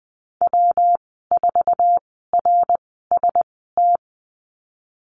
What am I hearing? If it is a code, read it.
Morse code, heard as a keyed tone, about 20 wpm, spaced as for 15 wpm, 700 Hz. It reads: W4RST
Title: 154 Lesson 25g - Letter p - Call Signs 20wpm